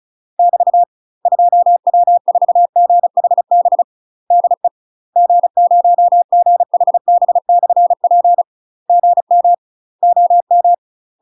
Morse code, heard as a keyed tone, about 36 wpm, 700 Hz. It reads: <BT> 2W4GHB DE G0GHB/P GM OM
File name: UK MASTER QSOs TRAINING / 35WPM / UK MASTER QSOs 0033